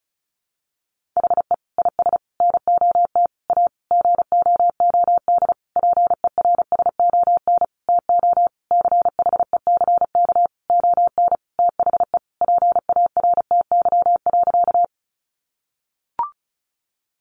Morse code, heard as a keyed tone, about 34 wpm, 700 Hz. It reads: HE IS NOT A GOOD PERSON TO CHECK ON THE PARTY.
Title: Sentences from Top 1400 Words 35wpm